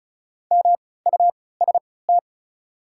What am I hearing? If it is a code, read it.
Morse code, heard as a keyed tone, about 36 wpm, spaced as for 20 wpm, 690 Hz. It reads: MUST
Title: 3x Character Spacing - Top 500 Words - 35wpm